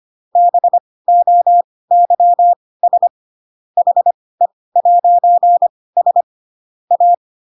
Morse code, heard as a keyed tone, about 25 wpm, spaced as for 18 wpm, 700 Hz. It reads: BOYS HE'S A
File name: Pinocchio ch3 12 25 ik1hge